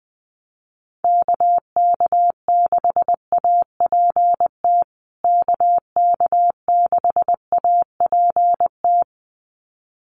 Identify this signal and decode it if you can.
Morse code, heard as a keyed tone, about 20 wpm, 700 Hz. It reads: KK6APT KK6APT
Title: USA Call Signs Repeated 2x 20wpm